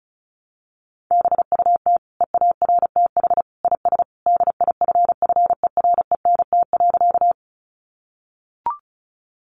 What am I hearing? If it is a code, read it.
Morse code, heard as a keyed tone, about 34 wpm, 700 Hz. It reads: BUT EARTH IS DIFFERENT.
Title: Sentences from Top 600 Words 35wpm